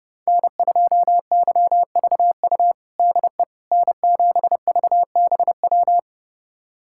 Morse code, heard as a keyed tone, about 30 wpm, 700 Hz. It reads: N2YVU DE N7VBW
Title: USA Callsign DE Callsign 30wpm